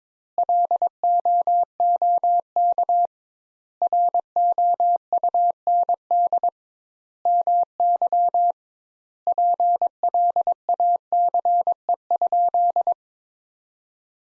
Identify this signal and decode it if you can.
Morse code, heard as a keyed tone, about 22 wpm, 700 Hz. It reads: LOOK ROUND MY PLACE?
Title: Sentences Top 200 Words - 22wpm